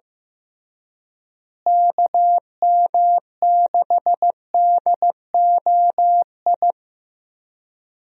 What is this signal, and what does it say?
Morse code, heard as a keyed tone, about 15 wpm, 700 Hz. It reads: KM6DOI
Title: USA Call Signs - 6 Characters - Repeated 3x - 15wpm